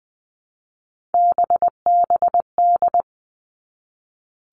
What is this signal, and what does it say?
Morse code, heard as a keyed tone, about 20 wpm, 700 Hz. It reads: BBD
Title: All Cut Numbers - 3-Digit 20wpm